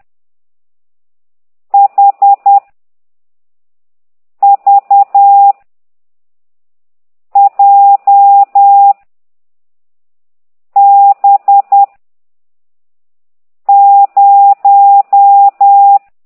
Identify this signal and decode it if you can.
Morse code, heard as a keyed tone, about 10 wpm, 800 Hz. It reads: H V J B 0
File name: ct2idw / exerc15